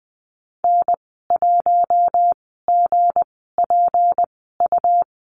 Morse code, heard as a keyed tone, about 20 wpm, spaced as for 15 wpm, 700 Hz. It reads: N1GPU
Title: 167 Lesson 27g - Number 7 - Call Signs 20wpm